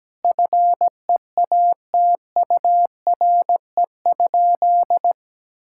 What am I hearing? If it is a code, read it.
Morse code, heard as a keyed tone, about 17 wpm, 700 Hz. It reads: FEATURE?
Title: Sentences from Top 1000 Words - Review Part 5 - 17wpm